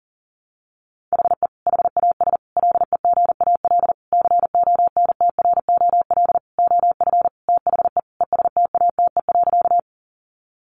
Morse code, heard as a keyed tone, about 40 wpm, 700 Hz. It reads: HE HAS LEGAL CONTROL OF THE ESTATE.
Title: Sentences from Top 1200 Words 40wpm